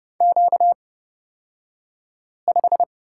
Morse code, heard as a keyed tone, about 31 wpm, spaced as for 5 wpm, 700 Hz. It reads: Q5
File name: Lesson 38d - Letter q - Two Characters 30wpm